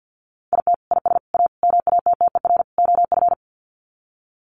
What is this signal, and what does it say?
Morse code, heard as a keyed tone, about 50 wpm, 700 Hz. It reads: IT IS A MATTER OF
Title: Sets of 5 Words Encore2 50wpm